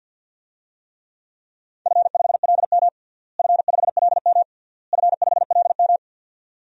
Morse code, heard as a keyed tone, about 50 wpm, 700 Hz. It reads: F5LM F5LM F5LM
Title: DXCC-100 Call Signs - 4 Characters - Repeated 3x - 50wpm